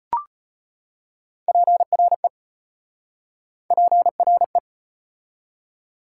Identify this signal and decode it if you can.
Morse code, heard as a keyed tone, about 33 wpm, 700 Hz. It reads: PRE PRE